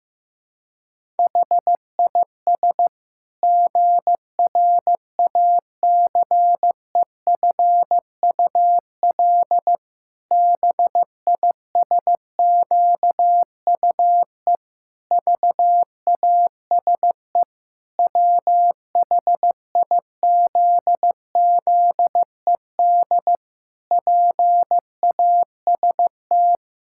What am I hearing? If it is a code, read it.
Morse code, heard as a keyed tone, about 15 wpm, 700 Hz. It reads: HIS GRACEFUL BISQUE VASE WHIZZED PAST
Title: Pangram Challenge v2 - 15wpm